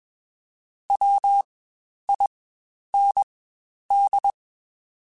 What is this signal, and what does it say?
Morse code, heard as a keyed tone, about 21 wpm, spaced as for 10 wpm, 800 Hz. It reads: WIND
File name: Spaced Characters / Spaced Characters-mp3s / Weather 4 21 sp @20